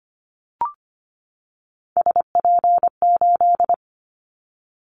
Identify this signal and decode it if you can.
Morse code, heard as a keyed tone, about 25 wpm, 700 Hz. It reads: SP8
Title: Callsign Prefixes for Top 25 DXCC - 25wpm